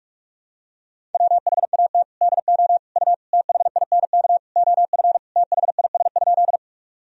Morse code, heard as a keyed tone, about 42 wpm, 700 Hz. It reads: WHAT DO U THINK OF THIS?
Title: Sentences from Top 200 Words and Abbreviations 45wpm